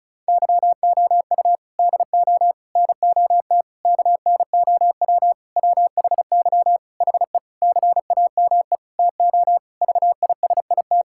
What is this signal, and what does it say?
Morse code, heard as a keyed tone, about 36 wpm, 700 Hz. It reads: YOU DO NOT KNOW WHY HE CAME TO VISIT
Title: Sentences from Top 1000 Words - Review Part 6 - 35wpm